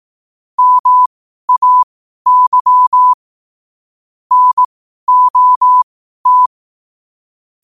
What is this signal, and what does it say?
Morse code, heard as a keyed tone, about 18 wpm, spaced as for 12 wpm, 1000 Hz. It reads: MAY NOT
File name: qotdfiles / 12wpm